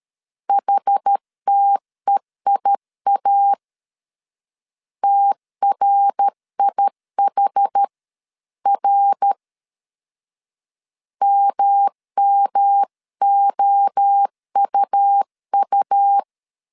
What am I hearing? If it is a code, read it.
Morse code, heard as a keyed tone, about 13 wpm, 800 Hz. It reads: HTEIA TRIHR MMOUU
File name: CW / DET / 1 3 1500C S A 12wpm 2905